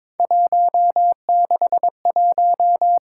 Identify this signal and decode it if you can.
Morse code, heard as a keyed tone, about 23 wpm, 700 Hz. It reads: 161
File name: practice / 22wpm / Three Letter-Number Combinations - 22wpm